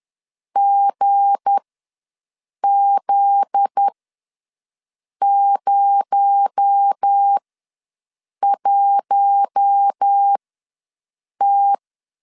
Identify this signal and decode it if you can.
Morse code, heard as a keyed tone, about 10 wpm, spaced as for 6 wpm, 800 Hz. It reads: GZ01T
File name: LEZIONI / 7 GF9Z / 7 GF9Z ws 10wpm 3919